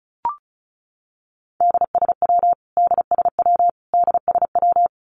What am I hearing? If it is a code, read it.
Morse code, heard as a keyed tone, about 34 wpm, 700 Hz. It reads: DSW DSW DSW
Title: Abbreviations Repeated 3x - 35wpm